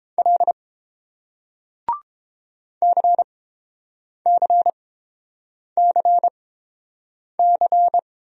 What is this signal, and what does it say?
Morse code, heard as a keyed tone, about 29 wpm, spaced as for 7 wpm, 700 Hz. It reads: L CCCC